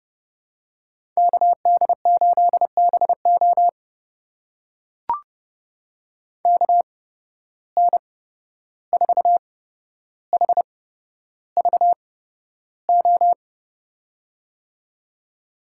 Morse code, heard as a keyed tone, about 31 wpm, spaced as for 12 wpm, 700 Hz. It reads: KD8BO KN4HVO
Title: USA Call Signs - 8x Character Spacing - 30wpm